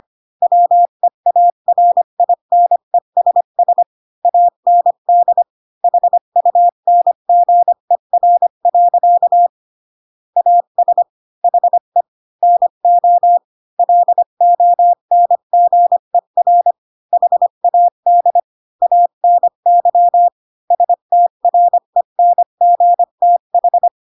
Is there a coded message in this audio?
Yes: Morse code from 0.4 to 23.9 s, heard as a keyed tone, about 25 wpm, 700 Hz. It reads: WEARINESS AND HUNGER. AS HE NO LONGER HAD ANY STRENGTH